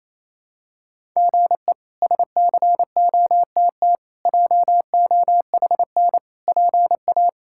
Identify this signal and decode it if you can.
Morse code, heard as a keyed tone, about 28 wpm, 700 Hz. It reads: GE SCOTT JOHN PA
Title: K1USN SST QSO Practice - 28wpm